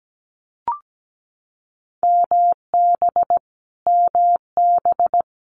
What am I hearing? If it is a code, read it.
Morse code, heard as a keyed tone, about 17 wpm, 700 Hz. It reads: MB MB
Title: Multi-letter Phonemes Repeated 3x 17wpm